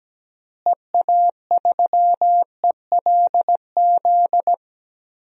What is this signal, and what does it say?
Morse code, heard as a keyed tone, about 17 wpm, 700 Hz. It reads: EA3ELZ
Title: DXCC-100 Call Signs - 6 Characters - Repeated 2x - 17wpm